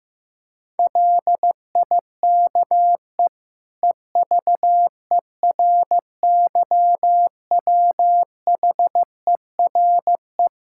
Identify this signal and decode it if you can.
Morse code, heard as a keyed tone, about 15 wpm, 700 Hz. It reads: LIKE EVERYWHERE
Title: Sentences from Top 1600 Words 15wpm